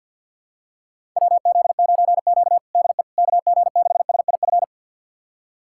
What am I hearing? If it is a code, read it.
Morse code, heard as a keyed tone, about 49 wpm, 700 Hz. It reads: WZ9Q DE KG6SIF
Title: USA Callsign DE Callsign 50wpm